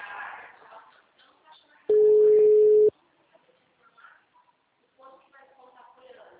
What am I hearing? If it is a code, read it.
Ringback tone (Europe, 425 Hz), with speech.
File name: out- -2006- - - .29967